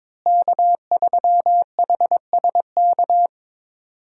Morse code, heard as a keyed tone, about 22 wpm, 700 Hz. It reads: K3HSK
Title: USA Call Signs - 5 Characters - Repeated 2x - 22wpm